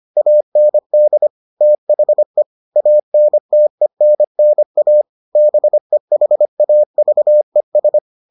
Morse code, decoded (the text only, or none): AND THE ANTENNA BEHAVES